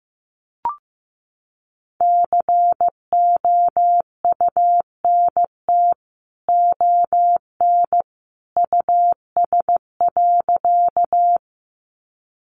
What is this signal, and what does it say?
Morse code, heard as a keyed tone, about 15 wpm, 700 Hz. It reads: COUNT ON US.
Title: Sentences from Top 1500 Words 15wpm